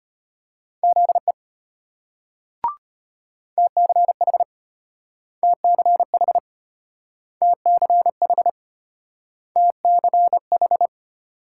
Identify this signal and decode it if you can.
Morse code, heard as a keyed tone, about 34 wpm, 700 Hz. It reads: ZE TCH TCH TCH TCH